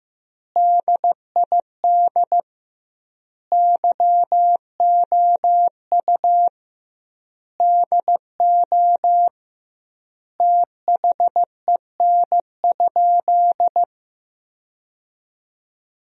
Morse code, heard as a keyed tone, about 16 wpm, 700 Hz. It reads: DID YOU DO THEN?
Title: 2x Word-Spacing - Sentences from Top 100 Words - 15wpm